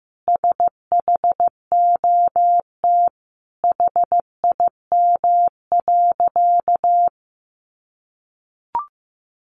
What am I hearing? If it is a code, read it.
Morse code, heard as a keyed tone, about 14 wpm, 700 Hz. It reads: SHOT HIM.
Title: Sentences from Top 1100 Words 15wpm